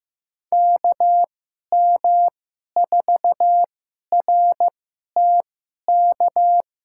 Morse code, heard as a keyed tone, about 15 wpm, spaced as for 11 wpm, 700 Hz. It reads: KM4RTK
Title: USA Call Signs - 6 Characters - 2x Character Spacing - 15wpm